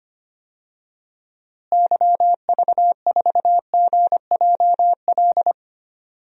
Morse code, heard as a keyed tone, about 25 wpm, 700 Hz. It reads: YV4GJL
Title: DXCC-100 Call Signs - 6 Characters - Repeated 2x - 25wpm